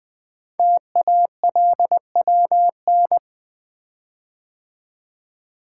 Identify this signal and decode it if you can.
Morse code, heard as a keyed tone, about 20 wpm, 700 Hz. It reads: TALWN
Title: Code Groups - 5 Letters - 20wpm